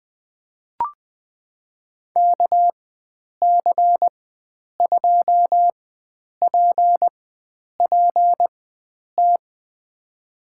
Morse code, heard as a keyed tone, about 20 wpm, spaced as for 9 wpm, 700 Hz. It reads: KC2PPT
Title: USA Call Signs - 6 Characters - 4x Character Spacing - 20wpm